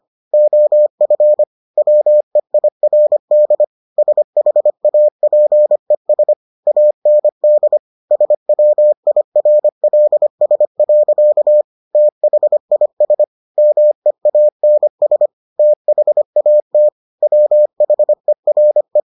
Morse code, heard as a keyed tone, about 25 wpm, 600 Hz. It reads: OF WEIRD SHAPES AND SWIRLS. THIS MEANS THAT WHERE